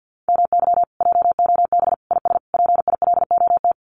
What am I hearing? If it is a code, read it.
Morse code, heard as a keyed tone, about 50 wpm, 700 Hz. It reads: MY JOB IS PILOT
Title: Common QSO Phrases Encore 50wpm